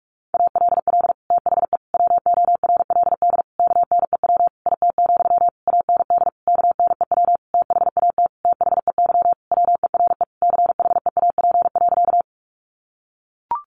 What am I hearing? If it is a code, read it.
Morse code, heard as a keyed tone, about 44 wpm, 700 Hz. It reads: ALL THE WORLD KNEW IT, AND KNEW THAT THEY WERE CHEAP.
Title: Sentences from Top 1200 Words 45wpm